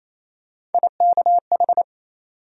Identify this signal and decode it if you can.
Morse code, heard as a keyed tone, about 28 wpm, 700 Hz. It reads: IKH
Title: USA Call Sign Postfixes - 3 Letters 28wpm